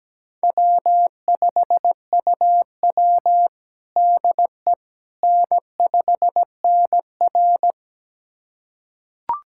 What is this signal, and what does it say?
Morse code, heard as a keyed tone, about 17 wpm, 700 Hz. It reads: W5UW DE N5NR
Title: USA Callsign DE Callsign - 4 Characters - 17wpm